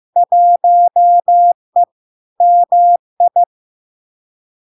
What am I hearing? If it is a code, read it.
Morse code, heard as a keyed tone, about 15 wpm, 700 Hz. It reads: 1E MI
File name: LARK-FD-2022-Log-in-CW-15wpm-700-Hz